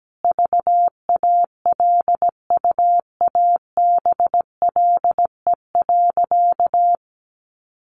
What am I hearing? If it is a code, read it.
Morse code, heard as a keyed tone, about 17 wpm, 700 Hz. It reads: VALUABLE.
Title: Sentences from Top 1100 Words 17wpm